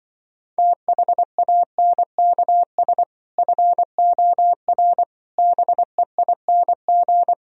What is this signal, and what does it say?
Morse code, heard as a keyed tone, about 25 wpm, 700 Hz. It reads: THANKS FOR BEING
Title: Speed-Racing - Sets of 5 Words 20wpm